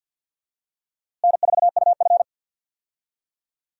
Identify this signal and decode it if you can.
Morse code, heard as a keyed tone, about 50 wpm, 700 Hz. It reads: N4UF
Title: USA Call Signs - 4 Characters - Repeated 2x - 50wpm